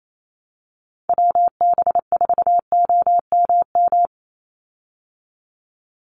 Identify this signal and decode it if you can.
Morse code, heard as a keyed tone, about 28 wpm, 700 Hz. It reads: WB4OMM
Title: Call Signs Repeated 2x - 28wpm